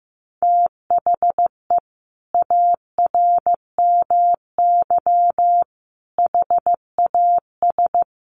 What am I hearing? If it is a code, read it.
Morse code, heard as a keyed tone, about 15 wpm, 700 Hz. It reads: THE ARMY HAS
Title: Sentences from Top 1300 Words 15wpm